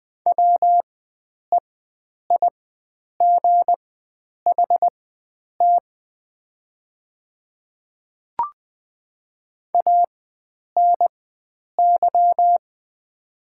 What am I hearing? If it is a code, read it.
Morse code, heard as a keyed tone, about 20 wpm, spaced as for 9 wpm, 700 Hz. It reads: WEIGHT ANY
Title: 4x Character Spacing - Top 500 Words - 20wpm